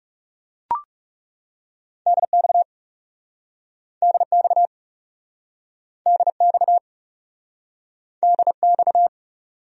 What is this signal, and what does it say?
Morse code, heard as a keyed tone, about 37 wpm, 700 Hz. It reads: DX DX DX DX
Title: Speed-Racing Abbreviations - 30wpm